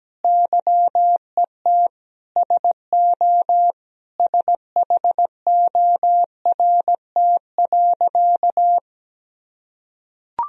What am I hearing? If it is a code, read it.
Morse code, heard as a keyed tone, about 17 wpm, 700 Hz. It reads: YET SO SHORT.
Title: Sentences from Top 500 Words and Abbreviations 17wpm